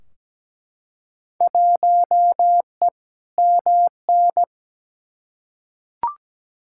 Morse code, heard as a keyed tone, about 17 wpm, 700 Hz. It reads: 1E MN
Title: Field Day Exchanges - 17wpm